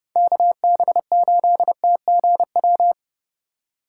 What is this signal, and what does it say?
Morse code, heard as a keyed tone, about 30 wpm, 700 Hz. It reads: KB8TGW